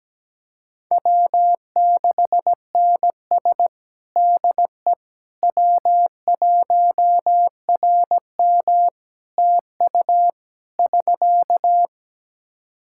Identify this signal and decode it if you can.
Morse code, heard as a keyed tone, about 17 wpm, 700 Hz. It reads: W6NS DE W1RM TU <SK>
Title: Practice QSO - Exchange 4 - Ending - 17wpm